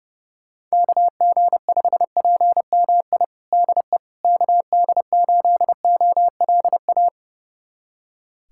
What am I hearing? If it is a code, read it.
Morse code, heard as a keyed tone, about 30 wpm, 700 Hz. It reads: KG5PMI DE KD8OLA
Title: USA Callsign DE Callsign - 6 Characters - 30wpm